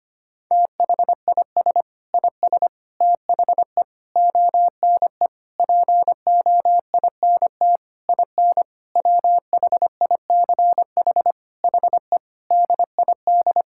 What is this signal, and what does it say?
Morse code, heard as a keyed tone, about 25 wpm, 700 Hz. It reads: THIS IS THE ONE POINT IN WHICH HE DID
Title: Sentences from Top 1000 Words - Review Part 5 - 25wpm